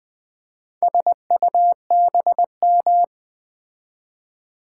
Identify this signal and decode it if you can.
Morse code, heard as a keyed tone, about 20 wpm, 700 Hz. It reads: SUBM
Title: Build A Word - 4 Letters At A Time - 10-Letter Words - 20wpm